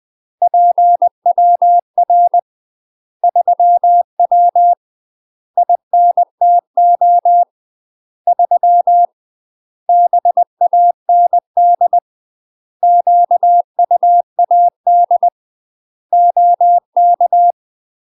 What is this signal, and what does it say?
Morse code, heard as a keyed tone, about 20 wpm, 700 Hz. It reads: PWR 3W INTO 3 BAND QUAD OK
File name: UK MASTER QSOs TRAINING / 20WPM / UK MASTER QSOs 0007